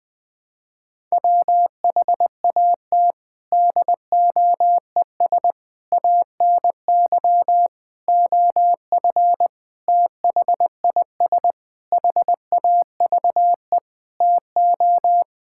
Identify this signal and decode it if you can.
Morse code, heard as a keyed tone, about 20 wpm, 700 Hz. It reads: WHAT DOES ANY OF THIS HAVE TO
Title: Sets of 9 Words - 20wpm